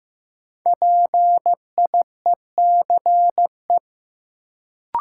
Morse code, heard as a keyed tone, about 15 wpm, 700 Hz. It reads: PIECE
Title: Sentences from Top 400 Words - New Word Drill - 15wpm